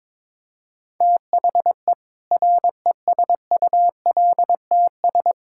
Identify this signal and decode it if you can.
Morse code, heard as a keyed tone, about 21 wpm, 700 Hz. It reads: THE RESULTS
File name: practice / 22wpm / Sentences with 7-Word Phrases - 22wpm